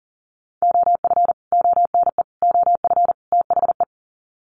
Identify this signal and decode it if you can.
Morse code, heard as a keyed tone, about 40 wpm, 700 Hz. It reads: OF ONE OF THE
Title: Sets of 4 Words 40wpm